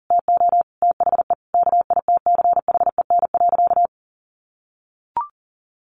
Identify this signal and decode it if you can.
Morse code, heard as a keyed tone, about 40 wpm, 700 Hz. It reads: TO THE KITCHEN.
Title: Sentences from Top 1400 Words 40wpm